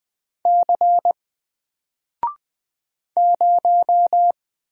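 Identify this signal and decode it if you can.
Morse code, heard as a keyed tone, about 20 wpm, 700 Hz. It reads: C 0
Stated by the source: Single letter and number Morse drill at 20 wpm, from N4LCD.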